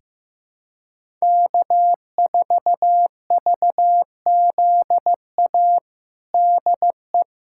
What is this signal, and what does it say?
Morse code, heard as a keyed tone, about 15 wpm, 700 Hz. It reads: K4VZA DE
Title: USA Callsign DE Callsign - 5 Characters - 15wpm